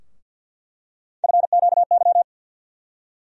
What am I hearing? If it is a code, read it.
Morse code, heard as a keyed tone, about 52 wpm, 700 Hz. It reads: FQY
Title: USA Call Sign Postfixes - 3 Letters 50wpm